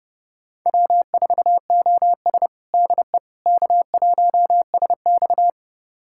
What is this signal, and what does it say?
Morse code, heard as a keyed tone, about 30 wpm, 700 Hz. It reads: W4OS DE K1SX
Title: USA Callsign DE Callsign - 4 Characters - 30wpm